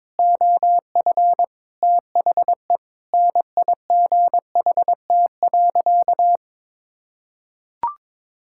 Morse code, heard as a keyed tone, about 22 wpm, 700 Hz. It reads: OF THE NIGHT.
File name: practice / 22wpm / Sentences with 3-Word Phrases - 22wpm